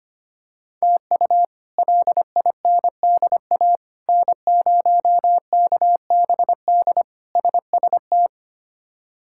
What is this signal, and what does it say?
Morse code, heard as a keyed tone, about 25 wpm, 700 Hz. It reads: TU LINDA N0KBD SST
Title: K1USN SST QSO Practice - 25wpm